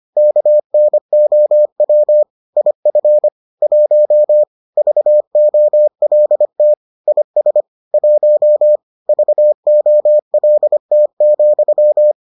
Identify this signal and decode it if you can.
Morse code, heard as a keyed tone, about 25 wpm, 600 Hz. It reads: KNOW IF 1 VOLT IS 1 VOLT,